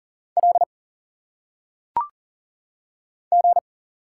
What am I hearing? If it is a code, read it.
Morse code, heard as a keyed tone, about 38 wpm, 700 Hz. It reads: L G
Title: Single Letters - Phonetic Pronunciation 40wpm